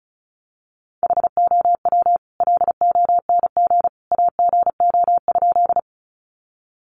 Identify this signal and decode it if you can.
Morse code, heard as a keyed tone, about 34 wpm, 700 Hz. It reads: HOW LONG AGO?
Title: Sentences from Top 500 Words 35wpm